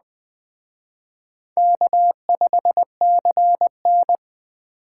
Morse code, heard as a keyed tone, about 20 wpm, 700 Hz. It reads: K5CN